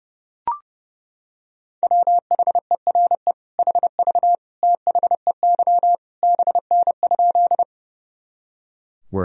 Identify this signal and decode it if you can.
Morse code, heard as a keyed tone, about 30 wpm, 700 Hz. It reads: WHERE HV THEY BN?
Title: Sentences from Top 200 Words and Abbreviations 30wpm